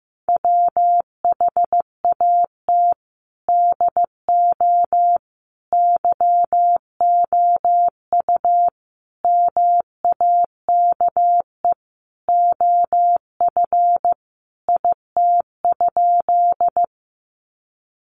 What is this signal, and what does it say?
Morse code, heard as a keyed tone, about 15 wpm, 700 Hz. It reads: WHAT DO YOU MAKE OF IT?
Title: Sentences from Top 100 Words 15wpm